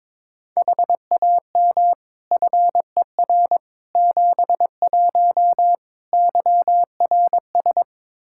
Morse code, heard as a keyed tone, about 22 wpm, 700 Hz. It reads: HAM FER 71 YRS
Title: Practice QSO - Exchange 3 - More Info - 22wpm